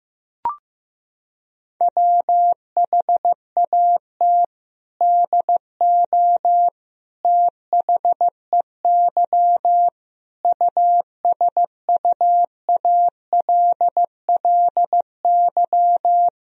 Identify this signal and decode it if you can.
Morse code, heard as a keyed tone, about 15 wpm, 700 Hz. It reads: WHAT DO THEY USUALLY
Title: Sentences from Top 1000 Words - Review Part 3 - 15wpm